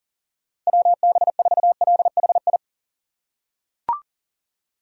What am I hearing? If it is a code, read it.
Morse code, heard as a keyed tone, about 40 wpm, 700 Hz. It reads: WB4LHI
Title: USA Call Signs - 6 Characters - 3x Character Spacing - 40wpm